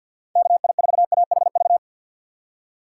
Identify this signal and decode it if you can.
Morse code, heard as a keyed tone, about 48 wpm, 700 Hz. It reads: KI4AHV
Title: USA Call Signs - 6 Characters - 4x Character Spacing - 50wpm